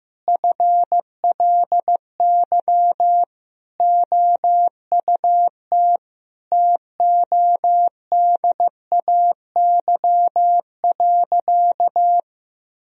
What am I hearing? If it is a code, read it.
Morse code, heard as a keyed tone, about 15 wpm, 700 Hz. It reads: FLY OUT TODAY.
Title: Sentences from Top 1000 Words - Review Part 2 - 15wpm